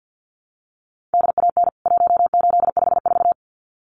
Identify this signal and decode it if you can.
Morse code, heard as a keyed tone, about 50 wpm, 700 Hz. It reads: DAN 1854
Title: CWT Exchange 50wpm